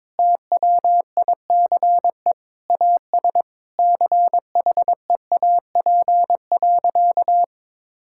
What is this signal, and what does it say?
Morse code, heard as a keyed tone, about 22 wpm, 700 Hz. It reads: TWICE AS CHEAP.
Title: Sentences from Top 1600 Words 22wpm